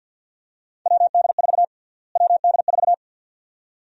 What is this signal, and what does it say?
Morse code, heard as a keyed tone, about 51 wpm, 700 Hz. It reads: WD4 WD4
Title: USA Call Sign Prefixes - Repeated 2x - 50wpm